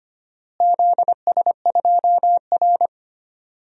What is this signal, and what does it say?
Morse code, heard as a keyed tone, about 25 wpm, 700 Hz. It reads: ZS2R
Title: DXCC-100 Call Signs - 4 Characters - Repeated 3x - 25wpm